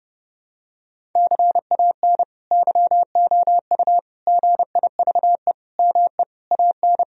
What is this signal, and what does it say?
Morse code, heard as a keyed tone, about 30 wpm, 700 Hz. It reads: CAN YOU GIVE ME AN
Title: Sentences from Top 1000 Words - Review Part 1 - 30wpm